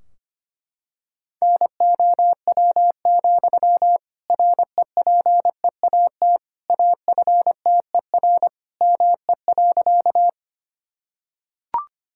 Morse code, heard as a keyed tone, about 25 wpm, 700 Hz. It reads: NOW, REPEAT AFTER ME.
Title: Sentences from Top 1600 Words 25wpm